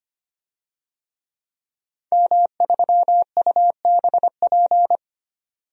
Morse code, heard as a keyed tone, about 25 wpm, 700 Hz. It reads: M3UBP
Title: DXCC-100 Call Signs - 5 Characters - 25wpm